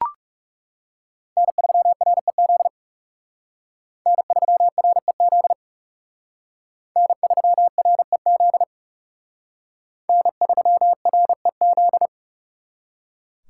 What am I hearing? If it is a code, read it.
Morse code, heard as a keyed tone, about 37 wpm, 700 Hz. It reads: N3REZ N3REZ N3REZ N3REZ
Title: Speed Racing - USA 1x3 Call Signs 30wpm